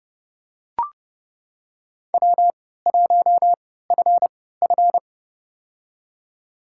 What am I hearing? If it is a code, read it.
Morse code, heard as a keyed tone, about 32 wpm, spaced as for 17 wpm, 700 Hz. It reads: W1FF
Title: USA Call Signs - 4 Characters - 3x Character Spacing - 30wpm